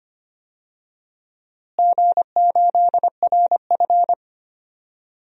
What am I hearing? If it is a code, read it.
Morse code, heard as a keyed tone, about 25 wpm, 700 Hz. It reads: G8RF
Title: DXCC-100 Call Signs - 4 Characters - Repeated 3x - 25wpm